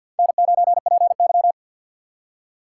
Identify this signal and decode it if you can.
Morse code, heard as a keyed tone, about 65 wpm, 700 Hz. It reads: N9PY